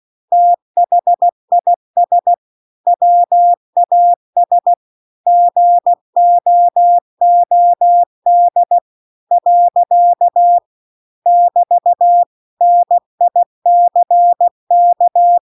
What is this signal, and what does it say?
Morse code, heard as a keyed tone, about 16 wpm, 700 Hz. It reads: THIS WAS GOOD . <BT> NICK